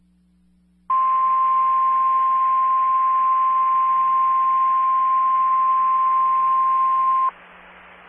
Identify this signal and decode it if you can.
Emergency Alert System (SAME) tones (the 1050 Hz attention tone). It is heard from 0.9 to 7.3 s.
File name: RR Skywarn / SWIRA Linked Repeaters and SKYWARN Spotters